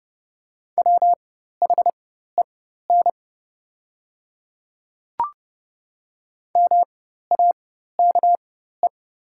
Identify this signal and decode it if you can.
Morse code, heard as a keyed tone, about 30 wpm, spaced as for 14 wpm, 700 Hz. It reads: WHEN MAKE
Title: 4x Character Spacing - Top 500 Words - 30wpm